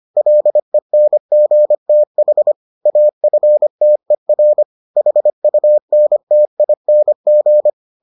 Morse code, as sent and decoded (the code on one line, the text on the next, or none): .-.. -. --. - .... / .- ..-. - . .-. / .... ..- -. - .. -. --.
LENGTH AFTER HUNTING